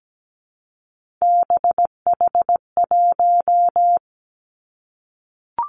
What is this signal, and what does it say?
Morse code, heard as a keyed tone, about 17 wpm, 700 Hz. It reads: BH1
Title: Three Challenging Letter-Number Combinations 17wpm